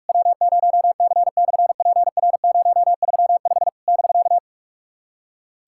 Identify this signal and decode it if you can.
Morse code, heard as a keyed tone, about 46 wpm, 690 Hz. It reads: W0C/PR035 <BK>
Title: SOTA QSO Practice 45wpm